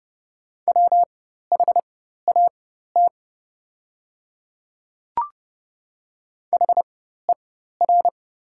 Morse code, heard as a keyed tone, about 30 wpm, spaced as for 14 wpm, 700 Hz. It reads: WHAT HER
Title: 4x Character Spacing - Top 100 Words - 30wpm